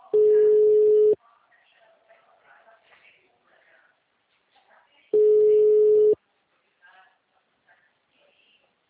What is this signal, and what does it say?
Ringback tone (Europe, 425 Hz), with speech.
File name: out- -2005- - - .42847